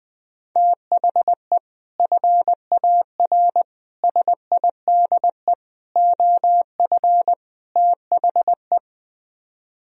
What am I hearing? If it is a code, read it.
Morse code, heard as a keyed tone, about 20 wpm, 700 Hz. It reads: THE FAR SIDE OF THE
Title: Speed-Racing - Sets of 5 Words 17wpm